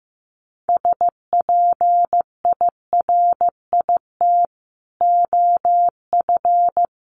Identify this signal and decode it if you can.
Morse code, heard as a keyed tone, about 14 wpm, 700 Hz. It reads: SPIRIT OF
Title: Sentences from Top 1400 Words 15wpm